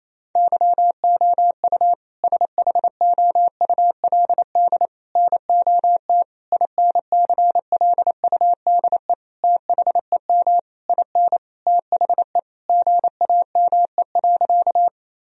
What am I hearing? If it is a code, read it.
Morse code, heard as a keyed tone, about 28 wpm, 700 Hz. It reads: YOU SHOULD NOT INCLUDE THEM IN THE GAME.
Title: Sentences from Top 1000 Words - Review Part 3 - 28wpm